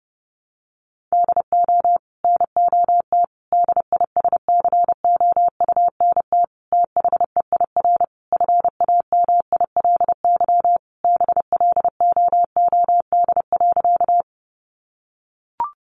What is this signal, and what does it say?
Morse code, heard as a keyed tone, about 28 wpm, 700 Hz. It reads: DO NOT DISCOUNT THEIR FAMILY BLOOD.
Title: Sentences from Top 1200 Words 30wpm